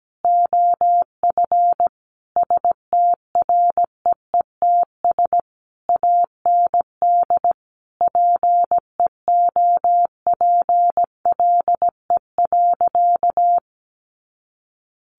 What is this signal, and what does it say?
Morse code, heard as a keyed tone, about 17 wpm, 700 Hz. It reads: OF STREETS AND PEOPLE.
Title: A Sound Of Thunder 17wpm